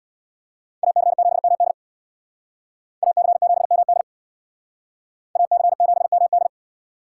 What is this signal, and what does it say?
Morse code, heard as a keyed tone, about 59 wpm, 700 Hz. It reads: AC7MD AC7MD AC7MD
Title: Speed Racing - USA 2x2 Call Signs 50wpm